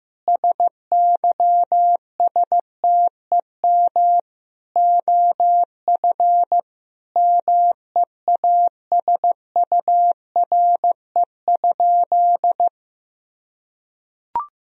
Morse code, heard as a keyed tone, about 15 wpm, 700 Hz. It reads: SYSTEM OF MEASURE?
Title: Sentences from Top 1000 Words - Review Part 3 - 15wpm